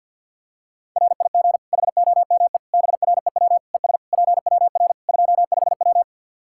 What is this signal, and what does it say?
Morse code, heard as a keyed tone, about 52 wpm, 700 Hz. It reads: RIG HOME BREW ES PWR 25W
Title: Practice QSO - Exchange 2 - Info - 50wpm